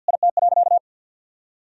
Morse code, heard as a keyed tone, about 50 wpm, 700 Hz. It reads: IT.
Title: Sentences from Top 1000 Words - Review Part 2 - 50wpm